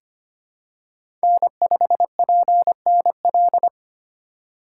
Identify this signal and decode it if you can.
Morse code, heard as a keyed tone, about 25 wpm, 690 Hz. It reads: N5PNL
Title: USA Call Signs - 4x Character Spacing - 25wpm